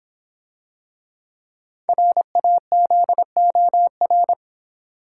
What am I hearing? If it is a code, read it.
Morse code, heard as a keyed tone, about 26 wpm, 700 Hz. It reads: RAZOR